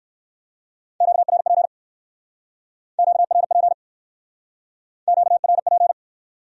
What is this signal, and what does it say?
Morse code, heard as a keyed tone, about 57 wpm, 700 Hz. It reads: QRP QRP QRP